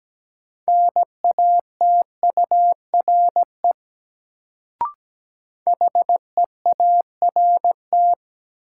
Morse code, heard as a keyed tone, about 17 wpm, 700 Hz. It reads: NATURE HEART
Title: Sets of 3 Words - New Word Drill - 17wpm